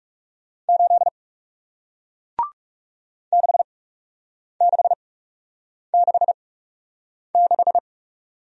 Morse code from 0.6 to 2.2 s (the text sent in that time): 8